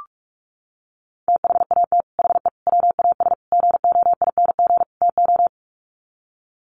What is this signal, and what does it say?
Morse code, heard as a keyed tone, about 44 wpm, 700 Hz. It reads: THAT HE WAS GOING TO
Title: Sets of 5 Words 45wpm